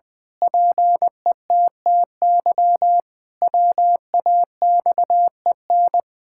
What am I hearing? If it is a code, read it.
Morse code, heard as a keyed tone, about 20 wpm, 700 Hz. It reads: PETTY WAXEN
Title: Pangram Challenge v2 - 20wpm